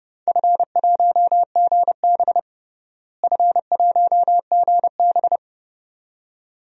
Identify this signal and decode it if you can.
Morse code, heard as a keyed tone, about 30 wpm, 700 Hz. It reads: F1GB F1GB
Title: DXCC-100 Call Signs - 4 Characters - Repeated 2x - 30wpm